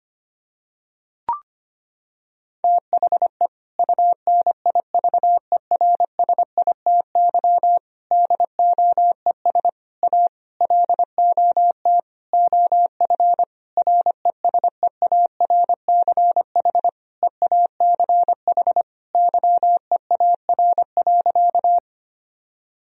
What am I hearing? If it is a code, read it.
Morse code, heard as a keyed tone, about 25 wpm, 700 Hz. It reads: THE UNIVERSITY DOES A LOT OF RESEARCH EACH YEAR.
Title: Sentences from Top 1000 Words - Review Part 4 - 25wpm